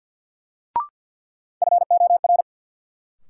Morse code, heard as a keyed tone, about 50 wpm, 700 Hz. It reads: FOR
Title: Sets of 2 Words - New Word Drill - 50wpm